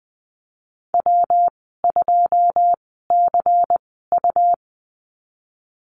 Morse code, heard as a keyed tone, about 20 wpm, spaced as for 15 wpm, 700 Hz. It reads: W2CU
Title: 128 Lesson 21g - Number 6 - Call Signs 20wpm